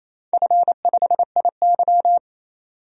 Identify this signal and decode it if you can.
Morse code, heard as a keyed tone, about 28 wpm, 700 Hz. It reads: F5IY
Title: DXCC-100 Call Signs - 4 Characters - 28wpm